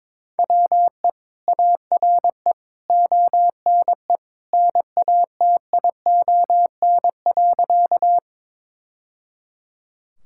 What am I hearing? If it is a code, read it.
Morse code, heard as a keyed tone, about 22 wpm, 700 Hz. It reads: WE ARE ONE NATION.
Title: Sentences from Top 1000 Words - Review Part 6 - 22wpm